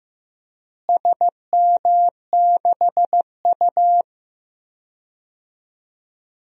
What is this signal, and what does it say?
Morse code, heard as a keyed tone, about 15 wpm, 700 Hz. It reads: SM6U
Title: DXCC-100 Call Signs - 4 Characters - Repeated 2x - 15wpm